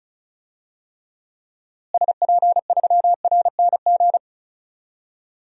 Morse code, heard as a keyed tone, about 34 wpm, 700 Hz. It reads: SP3RNG
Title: DXCC-100 Call Signs - 6 Characters - Repeated 3x - 35wpm